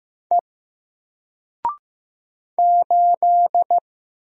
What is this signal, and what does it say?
Morse code, heard as a keyed tone, about 15 wpm, 700 Hz. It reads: E 8
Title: Single Letter-Number Alt 15wpm